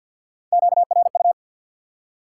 Morse code, heard as a keyed tone, about 47 wpm, 700 Hz. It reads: QRU
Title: Top 90 QSO Elements 50wpm